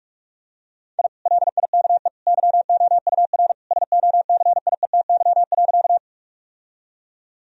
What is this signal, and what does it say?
Morse code, heard as a keyed tone, about 46 wpm, 700 Hz. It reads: I LIKE YOUR SOCIETY.
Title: Sentences from Top 1000 Words - Review Part 4 - 45wpm